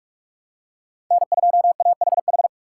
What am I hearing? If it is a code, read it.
Morse code, heard as a keyed tone, about 47 wpm, 700 Hz. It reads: N2AHH
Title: USA Call Signs - 5 Characters - 2x Character Spacing - 45wpm